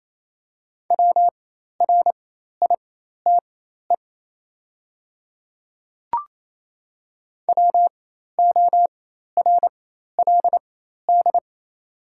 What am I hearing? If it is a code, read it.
Morse code, heard as a keyed tone, about 28 wpm, spaced as for 13 wpm, 700 Hz. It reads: WRITE WORLD
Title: 4x Character Spacing - Top 200 Words - 28wpm